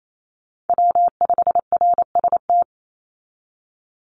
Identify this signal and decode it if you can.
Morse code, heard as a keyed tone, about 28 wpm, 700 Hz. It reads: W5RST
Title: NA SOTA Chaser Callsigns 28wpm